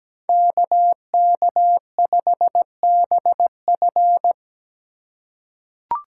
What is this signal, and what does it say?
Morse code, heard as a keyed tone, about 17 wpm, 700 Hz. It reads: KK5BF
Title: USA Call Signs - 5 Characters - 2x Character Spacing - 17wpm